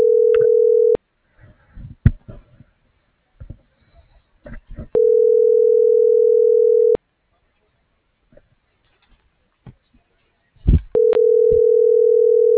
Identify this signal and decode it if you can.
Ringback tone (North America), 440 + 480 Hz.